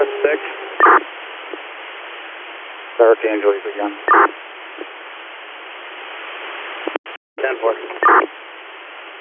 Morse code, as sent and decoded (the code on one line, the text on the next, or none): none